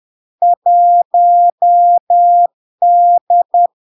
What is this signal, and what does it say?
Morse code, heard as a keyed tone, about 10 wpm, 700 Hz. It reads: 1D
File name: LARK-FD-2022-Log-in-CW-10wpm-700-Hz